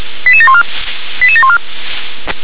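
Five-tone selective call (ZVEI), heard as quick runs of five tones, tones of about 70 ms. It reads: 88714 88714
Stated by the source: ZVEI five-tone selective call received on an SDR.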